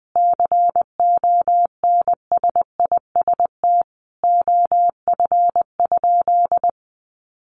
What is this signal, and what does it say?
Morse code, heard as a keyed tone, about 20 wpm, 700 Hz. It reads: CONSIST OF?
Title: Sentences from Top 1400 Words 20wpm